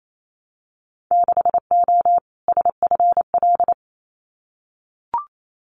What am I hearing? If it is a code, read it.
Morse code, heard as a keyed tone, about 28 wpm, 700 Hz. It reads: 6O SFL
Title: Winter Field Day Exchanges Encore 28wpm